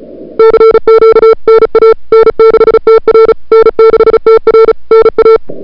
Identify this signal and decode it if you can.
Morse code, heard as a keyed tone, about 34 wpm, 440 Hz. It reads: CQ NA N6TR N6TR NA
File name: CWSprint / February 2007 CW Sprint / 01-2007 CW Sprint Hour1